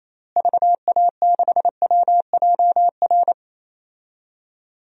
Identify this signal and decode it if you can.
Morse code, heard as a keyed tone, about 28 wpm, 700 Hz. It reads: VA6WJR
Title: DXCC-100 Call Signs - 6 Characters - Repeated 2x - 28wpm